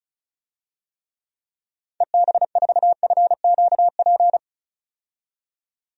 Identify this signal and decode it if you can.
Morse code, heard as a keyed tone, about 34 wpm, 700 Hz. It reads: EB4FQP
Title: DXCC-100 Call Signs - 6 Characters - 35wpm